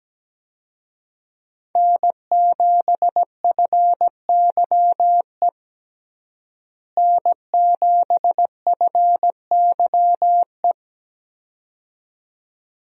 Morse code, heard as a keyed tone, about 17 wpm, 700 Hz. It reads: N7FYE N7FYE
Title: USA Call Signs - 5 Characters - Repeated 2x - 17wpm